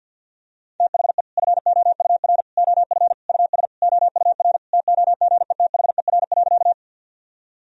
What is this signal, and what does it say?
Morse code, heard as a keyed tone, about 52 wpm, 700 Hz. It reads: THE FOUR OF US OUR TOGETHER.
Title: Sentences from Top 1000 Words - Review Part 1 - 50wpm